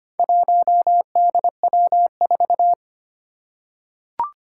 Morse code, heard as a keyed tone, about 25 wpm, 700 Hz. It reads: 1DW4
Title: Four Challenging Letter-Number Combinations 25wpm